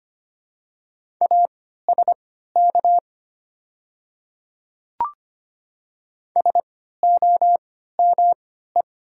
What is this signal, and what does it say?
Morse code, heard as a keyed tone, about 25 wpm, spaced as for 14 wpm, 690 Hz. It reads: ASK SOME
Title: 3x Character Spacing - Top 200 Words - 25wpm